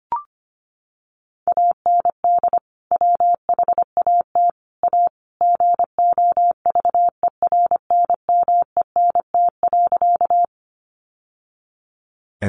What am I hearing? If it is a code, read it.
Morse code, heard as a keyed tone, about 25 wpm, 700 Hz. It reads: AND WHAT A GOVERNMENT.
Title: Sentences from Top 600 Words 25wpm